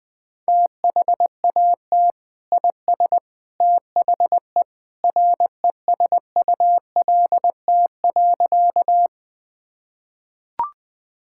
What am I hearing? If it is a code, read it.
Morse code, heard as a keyed tone, about 20 wpm, 700 Hz. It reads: THAT IS THE RESULT.
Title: Sentences from Top 1000 Words - Review Part 5 - 20wpm